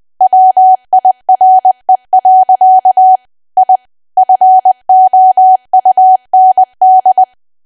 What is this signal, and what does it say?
Morse code, heard as a keyed tone, about 20 wpm, 750 Hz. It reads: WIRE. I FOUND
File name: cwa / W1AW-Level3 / 20WPM